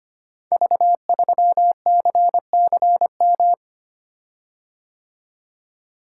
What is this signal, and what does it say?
Morse code, heard as a keyed tone, about 25 wpm, 700 Hz. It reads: V3CCM
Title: Code Groups - 5 Letter:Numbers - 25wpm